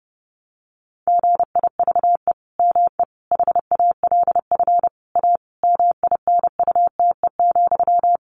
Morse code, heard as a keyed tone, about 30 wpm, 700 Hz. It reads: GIVE ME HALF A MINUTE,
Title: Sentences from Top 1500 Words 30wpm